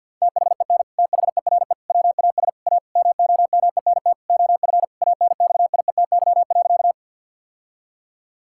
Morse code, heard as a keyed tone, about 46 wpm, 700 Hz. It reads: THEN THERE WAS A MOMENT OF ANXIETY.
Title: Sentences from Top 1600 Words 50wpm